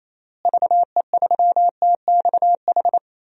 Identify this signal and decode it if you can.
Morse code, heard as a keyed tone, about 28 wpm, 700 Hz. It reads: VE3TXH
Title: DXCC-100 Call Signs - 6 Characters - Repeated 3x - 28wpm